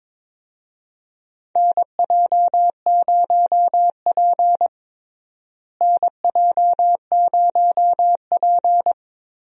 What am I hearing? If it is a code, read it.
Morse code, heard as a keyed tone, about 22 wpm, 700 Hz. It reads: NJ0P NJ0P
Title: USA Call Signs - 4 Characters - Repeated 3x - 22wpm